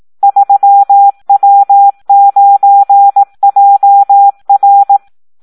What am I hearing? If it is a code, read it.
Morse code, heard as a keyed tone, about 18 wpm, 800 Hz. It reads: 3W9JR
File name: Audio practica CW / SD 1 18